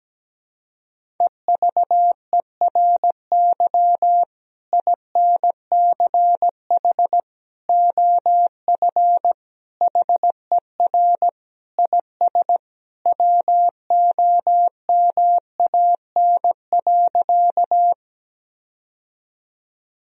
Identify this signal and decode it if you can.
Morse code, heard as a keyed tone, about 17 wpm, 700 Hz. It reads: EVERY INCH OF HER IS WOMAN.
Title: Sentences from Top 1000 Words - Review Part 3 - 17wpm